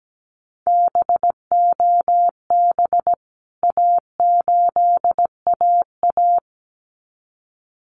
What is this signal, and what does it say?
Morse code, heard as a keyed tone, about 17 wpm, 700 Hz. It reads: BOB A8AA
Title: CWT Exchange 17wpm